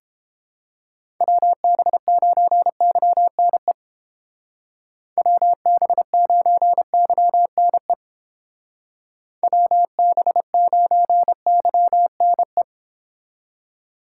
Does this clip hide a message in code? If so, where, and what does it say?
Yes: Morse code from 1.2 to 12.6 s, heard as a keyed tone, about 30 wpm, 700 Hz. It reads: WB9YNE WB9YNE WB9YNE